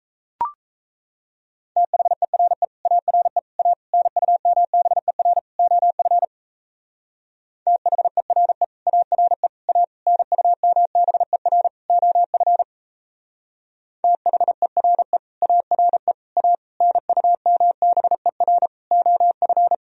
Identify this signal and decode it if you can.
Morse code, heard as a keyed tone, about 32 wpm, 700 Hz. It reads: THERE ARE A NUMBER OF THERE ARE A NUMBER OF THERE ARE A NUMBER OF
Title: Speed-Racing - Sets of 5 Words 28wpm